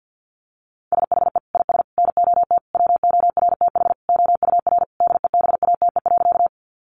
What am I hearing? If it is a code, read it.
Morse code, heard as a keyed tone, about 50 wpm, 700 Hz. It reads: SHE IS NOT WORTH OUR DEBATE.
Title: Sentences from Top 1100 Words 50wpm